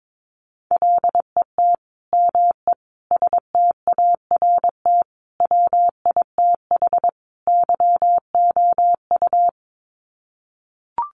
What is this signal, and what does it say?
Morse code, heard as a keyed tone, about 22 wpm, 700 Hz. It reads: LET ME START WITH YOU
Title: Sets of 5 Words Encore 22wpm